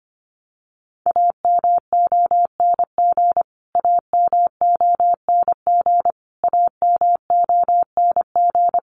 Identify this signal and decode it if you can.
Morse code, heard as a keyed tone, about 25 wpm, 700 Hz. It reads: AMONG AMONG AMONG
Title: Top 500 Words Repeated 25wpm